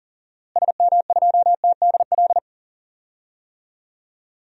Morse code, heard as a keyed tone, about 39 wpm, 700 Hz. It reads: SM2TDL
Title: DXCC-100 Call Signs - 6 Characters - Repeated 3x - 40wpm